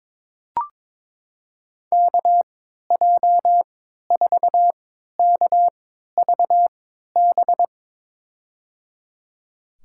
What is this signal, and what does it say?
Morse code, heard as a keyed tone, about 22 wpm, spaced as for 13 wpm, 700 Hz. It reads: KJ4KVB
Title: USA Call Signs - 3x Character Spacing - 22wpm